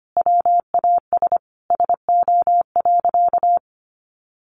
Morse code, heard as a keyed tone, about 25 wpm, 700 Hz. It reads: WAS SO.
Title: Sentences from Top 1100 Words 25wpm